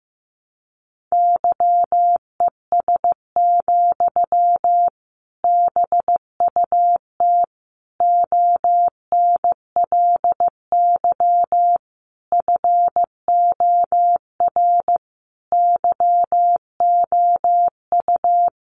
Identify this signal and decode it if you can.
Morse code, heard as a keyed tone, about 14 wpm, 700 Hz. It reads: YES, BUT ONLY FOR YOU
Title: Sentences from Top 500 Words 15wpm